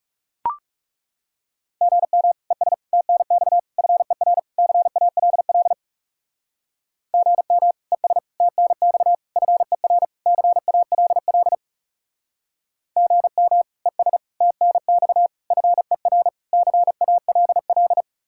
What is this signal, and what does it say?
Morse code, heard as a keyed tone, about 36 wpm, 700 Hz. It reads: GM ES TNX FER CALL GM ES TNX FER CALL GM ES TNX FER CALL
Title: Speed-Racing Common QSO Phrases - 30wpm